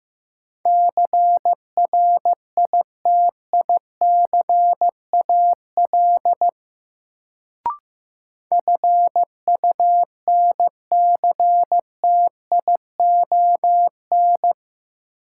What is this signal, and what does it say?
Morse code, heard as a keyed tone, about 15 wpm, 700 Hz. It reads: CRITICAL FUNCTION
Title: Sentences from Top 1100 Words - New Word Drill - 15wpm